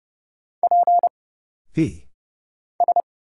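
Morse code, heard as a keyed tone, about 30 wpm, 700 Hz. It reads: P S
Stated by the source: Morse Code Ninja single-character practice (letters, numbers, punctuation, prosigns) at 30 wpm, by Kurt Zoglmann (AD0WE).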